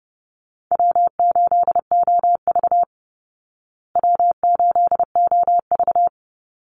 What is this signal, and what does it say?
Morse code, heard as a keyed tone, about 30 wpm, 700 Hz. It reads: W8OV W8OV
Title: Easy Call Signs Repeated 3x - 30wpm